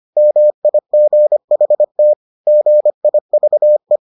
Morse code, heard as a keyed tone, about 25 wpm, 600 Hz. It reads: MIGHT GIVE